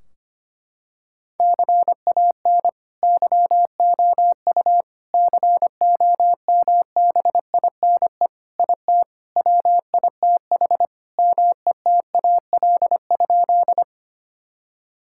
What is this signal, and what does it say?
Morse code, heard as a keyed tone, about 25 wpm, 700 Hz. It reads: CAN YOU COMBINE IT WITH METAL?
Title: Sentences from Top 1600 Words 25wpm